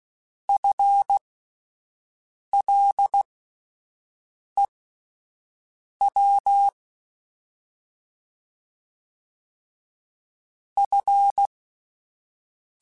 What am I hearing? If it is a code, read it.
Morse code, heard as a keyed tone, about 16 wpm, spaced as for 5 wpm, 800 Hz. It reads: FLEW F
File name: Alphabetical / Alphabetical mp3s / Fs @5